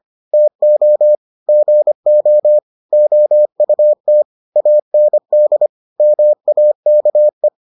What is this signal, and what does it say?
Morse code, heard as a keyed tone, about 25 wpm, 600 Hz. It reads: TO GO OUT AND MAKE